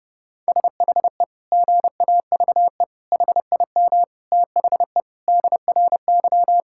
Morse code, heard as a keyed tone, about 30 wpm, 700 Hz. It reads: SHE GAVE HIM THE DRY
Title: Sentences from Top 1600 Words 30wpm